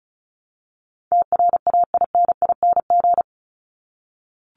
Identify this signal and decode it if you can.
Morse code, heard as a keyed tone, about 34 wpm, 700 Hz. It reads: TRAINING
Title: Top 501-750 Words 35wpm